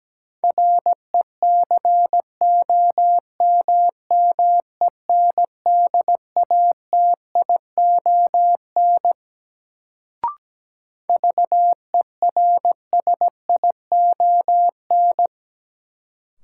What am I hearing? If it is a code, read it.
Morse code, heard as a keyed tone, about 17 wpm, 700 Hz. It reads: RECOMMENDATION VERSION
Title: Sentences from Top 1400 Words - New Word Drill - 17wpm